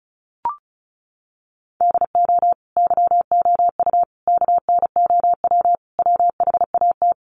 Morse code, heard as a keyed tone, about 34 wpm, 700 Hz. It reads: DO YOU KNOW WHAT
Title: Sentences from Top 100 Words 35wpm